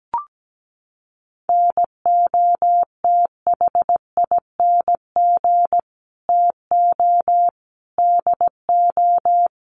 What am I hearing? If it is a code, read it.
Morse code, heard as a keyed tone, about 17 wpm, 700 Hz. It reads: NOTHING TO DO
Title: Sets of 5 Words Encore 17wpm